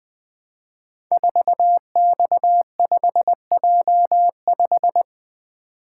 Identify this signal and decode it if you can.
Morse code, heard as a keyed tone, about 20 wpm, 700 Hz. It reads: 4X5J5
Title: Code Groups - 5 Letter:Numbers - 20wpm